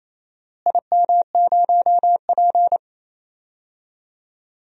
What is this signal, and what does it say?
Morse code, heard as a keyed tone, about 27 wpm, 700 Hz. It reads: IM0P
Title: DXCC-100 Call Signs - 4 Characters - Repeated 3x - 28wpm